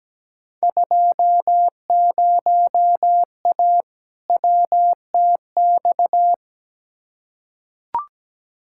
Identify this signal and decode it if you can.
Morse code, heard as a keyed tone, about 17 wpm, 700 Hz. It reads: 20A WTX
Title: Field Day Exchanges Encore - 17wpm